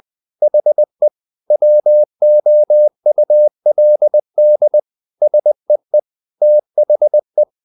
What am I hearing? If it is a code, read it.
Morse code, heard as a keyed tone, about 20 wpm, 600 Hz. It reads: HE WOULD SEE THE
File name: the old man and the sea / the old man and the sea part 1 20wpm